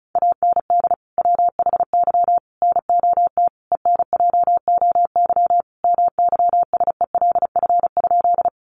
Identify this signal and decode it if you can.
Morse code, heard as a keyed tone, about 34 wpm, 700 Hz. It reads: AND WHY NOT ENJOY MYSELF?
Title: Sentences from Top 1300 Words 35wpm